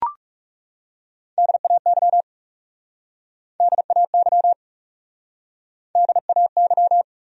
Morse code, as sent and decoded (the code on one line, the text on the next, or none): -.. .- -.-- / -.. .- -.-- / -.. .- -.--
DAY DAY DAY